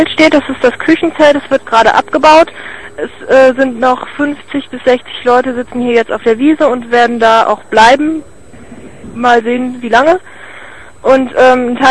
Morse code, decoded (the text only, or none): none